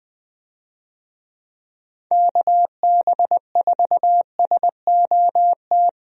Morse code, heard as a keyed tone, about 21 wpm, 700 Hz. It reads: KB4SOT
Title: USA Call Signs - 6 Characters - Repeated 2x - 20wpm